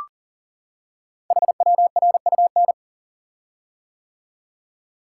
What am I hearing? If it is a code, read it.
Morse code, heard as a keyed tone, about 39 wpm, 700 Hz. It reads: HWRUN
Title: Code Groups - 5 Letters - 40wpm